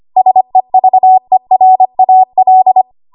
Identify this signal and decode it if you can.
Morse code, heard as a keyed tone, about 25 wpm, 750 Hz. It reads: SEVERAL
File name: Archive / 25 WPM / 25WPM